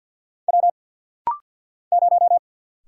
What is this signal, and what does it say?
Morse code, heard as a keyed tone, about 50 wpm, 690 Hz. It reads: W 0
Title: Single Letter-Number - Rapid-Fire 50wpm